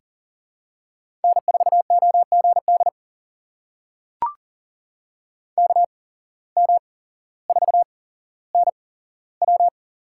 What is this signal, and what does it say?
Morse code, heard as a keyed tone, about 40 wpm, spaced as for 17 wpm, 700 Hz. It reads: N4OGD KM4NW
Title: USA Call Signs - 5 Characters - 8x Character Spacing - 40wpm